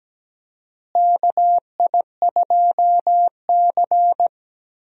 Morse code, heard as a keyed tone, about 17 wpm, 700 Hz. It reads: KI2C
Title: USA Call Signs - 4 Characters - Repeated 2x - 17wpm